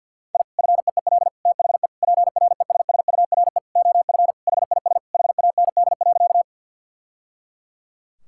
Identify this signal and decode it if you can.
Morse code, heard as a keyed tone, about 52 wpm, 700 Hz. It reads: I FEEL THE PRESSURE OF HIS HAND.
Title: Sentences from Top 1000 Words - Review Part 6 - 50wpm